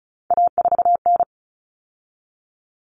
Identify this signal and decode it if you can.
Morse code, heard as a keyed tone, about 34 wpm, 700 Hz. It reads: A4N
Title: Common Cut Numbers - 3-Digit 35wpm